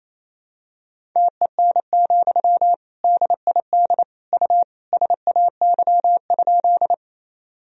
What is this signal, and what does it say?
Morse code, heard as a keyed tone, about 28 wpm, 700 Hz. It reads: TEN, DID U SAY?
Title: Sentences from Top 400 Words and Abbreviations 28wpm